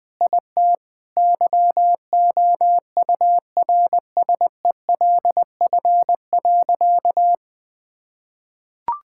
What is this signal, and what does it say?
Morse code, heard as a keyed tone, about 20 wpm, 700 Hz. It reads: IT YOURSELF.
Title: Sentences with 7-Word Phrases - Spoken Beginning - 20wpm